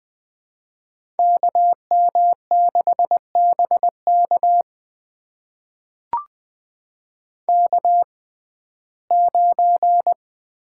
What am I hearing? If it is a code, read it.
Morse code, heard as a keyed tone, about 20 wpm, spaced as for 14 wpm, 700 Hz. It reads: KM6BK K9
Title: USA Call Signs - 5 Characters - 6x Character Spacing - 20wpm